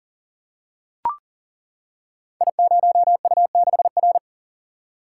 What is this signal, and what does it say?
Morse code, heard as a keyed tone, about 40 wpm, 700 Hz. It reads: I0U6R
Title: Code Groups - 5 Letter:Numbers - 40wpm